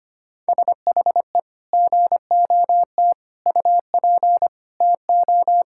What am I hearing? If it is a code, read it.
Morse code, heard as a keyed tone, about 25 wpm, 700 Hz. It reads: SHE GOT UP TO
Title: Sentences from Top 1600 Words 25wpm